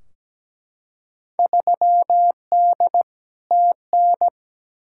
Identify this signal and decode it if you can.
Morse code, heard as a keyed tone, about 17 wpm, 700 Hz. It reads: 3D TN
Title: Field Day Exchanges Encore - 17wpm